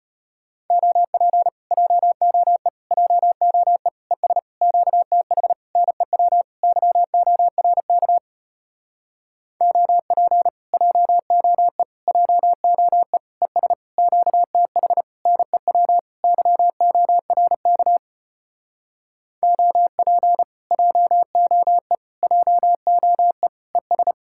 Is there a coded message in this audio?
Yes: Morse code from 0.7 to 24.1 s, heard as a keyed tone, about 33 wpm, 700 Hz. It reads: OP JOE JOE ES QTH NEW YORK OP JOE JOE ES QTH NEW YORK OP JOE JOE ES